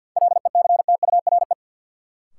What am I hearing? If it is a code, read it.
Morse code, heard as a keyed tone, about 53 wpm, 700 Hz. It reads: LECTURE
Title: Sentences from Top 1600 Words - New Word Drill - 50wpm